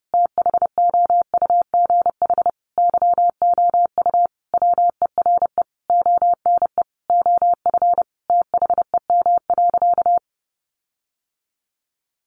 Morse code, heard as a keyed tone, about 30 wpm, 700 Hz. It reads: THOUGH YOU WERE ONE OF THEM.
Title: Sentences from Top 1500 Words 30wpm